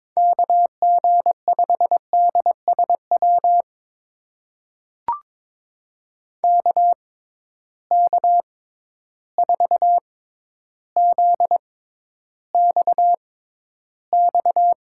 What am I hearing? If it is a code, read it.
Morse code, heard as a keyed tone, about 22 wpm, spaced as for 11 wpm, 700 Hz. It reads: KG5DSW KK4ZXX
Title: USA Call Signs - 6 Characters - 6x Character Spacing - 22wpm